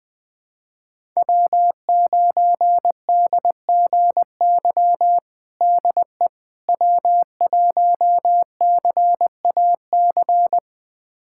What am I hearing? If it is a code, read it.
Morse code, heard as a keyed tone, about 20 wpm, 700 Hz. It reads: W9DGY DE W1CAC
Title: USA Callsign DE Callsign - 5 Characters - 20wpm